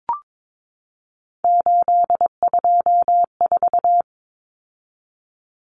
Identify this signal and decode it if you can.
Morse code, heard as a keyed tone, about 22 wpm, 700 Hz. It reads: 824
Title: Common Cut Numbers - 3-Digit 22wpm